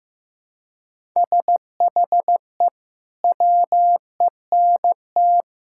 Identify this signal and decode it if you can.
Morse code, heard as a keyed tone, about 16 wpm, 700 Hz. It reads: SHE WENT
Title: Sentences from Top 1000 Words - Review Part 6 - 15wpm